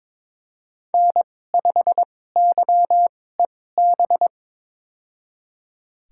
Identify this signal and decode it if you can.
Morse code, heard as a keyed tone, about 21 wpm, spaced as for 16 wpm, 700 Hz. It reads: N5YEB
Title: USA Call Signs - 5 Characters - 2x Character Spacing - 22wpm